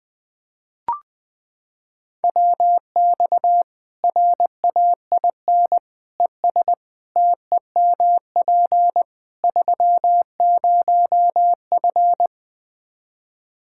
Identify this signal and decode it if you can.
Morse code, heard as a keyed tone, about 20 wpm, 700 Hz. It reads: WX RAIN ES TEMP 30F
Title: K1USN SST QSO Practice 20wpm